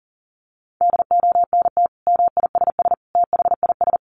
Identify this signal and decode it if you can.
Morse code, heard as a keyed tone, about 40 wpm, 700 Hz. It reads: DONT MISS THIS
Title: Sentences from Top 800 Words 40wpm